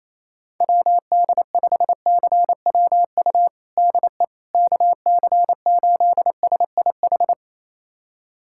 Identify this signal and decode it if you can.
Morse code, heard as a keyed tone, about 28 wpm, 700 Hz. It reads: WD5CWU DE KC8SIH
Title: USA Callsign DE Callsign 28wpm